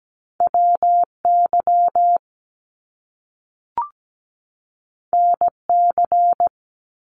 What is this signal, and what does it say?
Morse code, heard as a keyed tone, about 17 wpm, 700 Hz. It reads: WY NC
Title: US States 17wpm